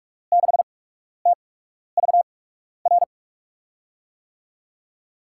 Morse code, heard as a keyed tone, about 44 wpm, spaced as for 12 wpm, 700 Hz. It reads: 6TVR
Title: USA Call Signs - 6 Characters - 8x Character Spacing - 45wpm